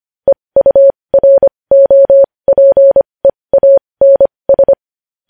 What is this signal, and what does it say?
Morse code, heard as a keyed tone, about 25 wpm, spaced as for 20 wpm, 550 Hz. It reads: EUROPEANS
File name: podcast / News headlines-1x-25x20